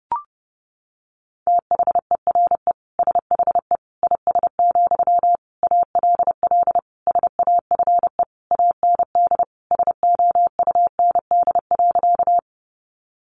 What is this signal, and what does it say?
Morse code, heard as a keyed tone, about 30 wpm, 700 Hz. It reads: THERE SHE IS, ALL SAFE AND SOUND.
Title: Sentences from Top 900 Words 30wpm